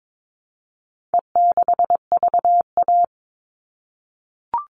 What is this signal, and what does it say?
Morse code, heard as a keyed tone, about 22 wpm, 700 Hz. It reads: E6VA
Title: All Cut Numbers - 4-Digit 22wpm